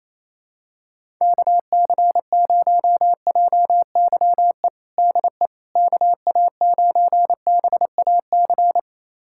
Morse code, heard as a keyed tone, about 28 wpm, 700 Hz. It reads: KC0JYE DE KA9BAC
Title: USA Callsign DE Callsign - 6 Characters - 28wpm